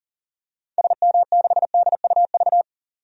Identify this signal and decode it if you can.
Morse code, heard as a keyed tone, about 37 wpm, 700 Hz. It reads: SM6DUV
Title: DXCC-100 Call Signs - 6 Characters - 40wpm